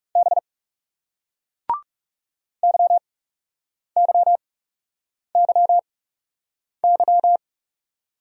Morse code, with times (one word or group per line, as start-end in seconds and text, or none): 0.2-0.4 B
2.6-7.4 YYYY